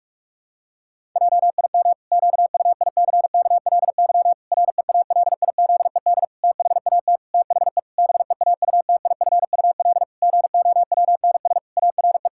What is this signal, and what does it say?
Morse code, heard as a keyed tone, about 41 wpm, 700 Hz. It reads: JIM QUICKLY REALIZED THAT THE BEAUTIFUL GOWNS ARE
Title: Pangram Challenge v2 - 45wpm